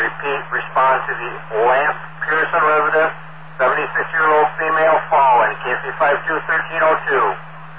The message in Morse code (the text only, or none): none